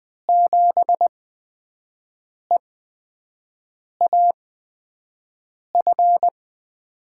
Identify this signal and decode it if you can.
Morse code, heard as a keyed tone, about 20 wpm, spaced as for 5 wpm, 700 Hz. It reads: 7EAF
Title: USA Call Signs - 8x Character Spacing - 20wpm